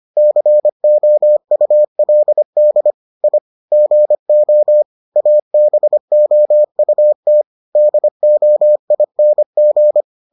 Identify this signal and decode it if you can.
Morse code, heard as a keyed tone, about 25 wpm, 600 Hz. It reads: COULD I GO ABOUT DOING